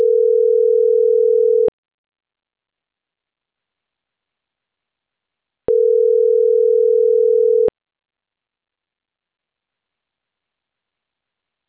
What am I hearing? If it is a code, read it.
Ringback tone (North America), 440 + 480 Hz.